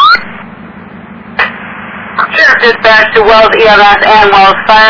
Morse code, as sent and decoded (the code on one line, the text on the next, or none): none